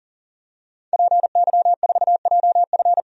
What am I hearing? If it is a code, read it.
Morse code, heard as a keyed tone, about 40 wpm, 700 Hz. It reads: PY4JF
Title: DXCC-100 Call Signs - 5 Characters - 40wpm